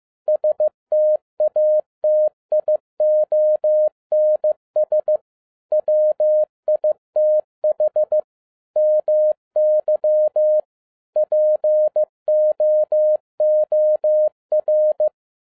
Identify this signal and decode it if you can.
Morse code, heard as a keyed tone, about 15 wpm, 610 Hz. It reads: STATIONS WITH MY POOR
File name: 2011-07-mp3 / n42 EXCITING QRP CW OPERATION 15wpm